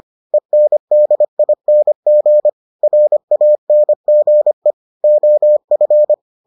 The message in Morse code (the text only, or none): ENDING RANGE OF